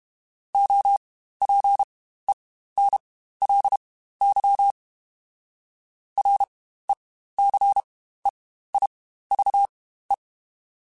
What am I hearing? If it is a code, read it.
Morse code, heard as a keyed tone, about 32 wpm, spaced as for 15 wpm, 800 Hz. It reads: OPENLY RECEIVE
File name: Quotes Spaced / Quotes Spaced-mp3s / Advice sp @30